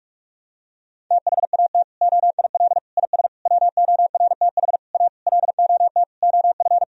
Morse code, heard as a keyed tone, about 46 wpm, 700 Hz. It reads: THAT OIL IS WORTH A LOT OF
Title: Sentences from Top 1000 Words - Review Part 4 - 45wpm